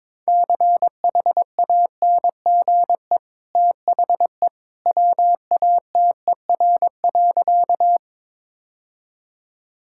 Morse code, heard as a keyed tone, about 22 wpm, 700 Hz. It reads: CHANGE THE WATER.
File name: practice / 22wpm / Easy Sentences with Hints - 22wpm